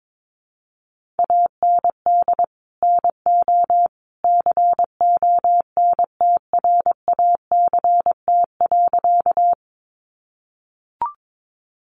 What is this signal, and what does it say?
Morse code, heard as a keyed tone, about 22 wpm, 700 Hz. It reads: AND NO CONTRACT.
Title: Sentences from Top 1400 Words 22wpm